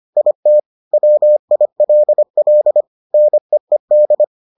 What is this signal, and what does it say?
Morse code, heard as a keyed tone, about 25 wpm, 600 Hz. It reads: IT WILL NEED